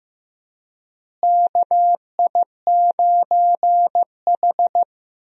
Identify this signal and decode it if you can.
Morse code, heard as a keyed tone, about 15 wpm, 700 Hz. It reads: KI9H
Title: USA Call Signs - 4 Characters - Repeated 2x - 15wpm